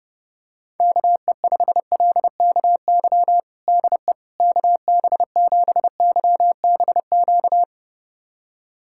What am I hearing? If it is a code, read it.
Morse code, heard as a keyed tone, about 30 wpm, 700 Hz. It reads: KE5LKY DE KB7YBQ
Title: USA Callsign DE Callsign - 6 Characters - 30wpm